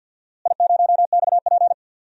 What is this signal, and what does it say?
Morse code, heard as a keyed tone, about 47 wpm, 700 Hz. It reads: I0XP
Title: DXCC-100 Call Signs - 4 Characters - Repeated 2x - 50wpm